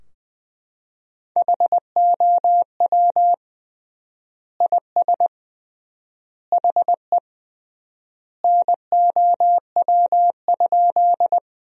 Morse code, heard as a keyed tone, about 21 wpm, 700 Hz. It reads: HOW IS HE NOW?
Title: 3x Word-Spacing - Sentences from Top 100 Words - 20wpm